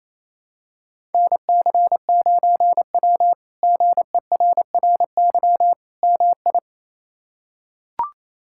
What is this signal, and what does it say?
Morse code, heard as a keyed tone, about 28 wpm, 700 Hz. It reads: NC9W GERRY MI
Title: K1USN SST QSO Practice - 28wpm